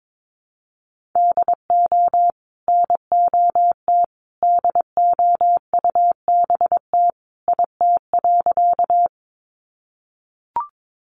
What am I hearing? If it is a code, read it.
Morse code, heard as a keyed tone, about 22 wpm, 700 Hz. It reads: DO NOT DOUBT IT.
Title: Sentences from Top 1500 Words 22wpm